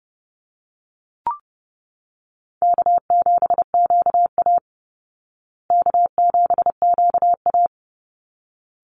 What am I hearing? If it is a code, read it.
Morse code, heard as a keyed tone, about 30 wpm, 700 Hz. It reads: K7QA K7QA
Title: Easy Call Signs Repeated 2x - 30wpm